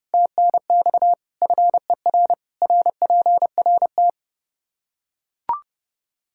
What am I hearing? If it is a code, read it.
Morse code, heard as a keyed tone, about 32 wpm, 700 Hz. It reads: TNX FER RPRT
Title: K1USN SST QSO Practice 30wpm